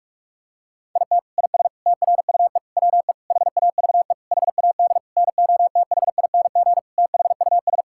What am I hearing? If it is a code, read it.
Morse code, heard as a keyed tone, about 47 wpm, 700 Hz. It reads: IT IS TRUE WE HAVE HAD NOTHING THUS
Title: Sentences from Top 1000 Words - Review Part 5 - 45wpm